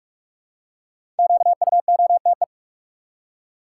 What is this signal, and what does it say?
Morse code, heard as a keyed tone, about 46 wpm, 700 Hz. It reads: QUOTE
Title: Five Letter Words - 45wpm